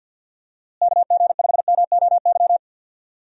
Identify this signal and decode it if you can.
Morse code, heard as a keyed tone, about 53 wpm, 700 Hz. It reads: KG5MOY
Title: USA Call Signs - 6 Characters - Repeated 2x - 50wpm